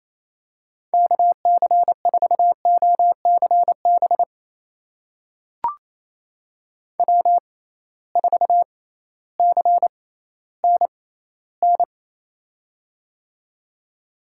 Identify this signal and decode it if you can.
Morse code, heard as a keyed tone, about 28 wpm, spaced as for 15 wpm, 700 Hz. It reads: KC4OCB W4CNN
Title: USA Call Signs - 6x Character Spacing - 28wpm